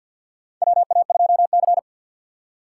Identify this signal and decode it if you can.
Morse code, heard as a keyed tone, about 53 wpm, 700 Hz. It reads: WA2C
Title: USA Call Signs - 4 Characters - 2x Character Spacing - 50wpm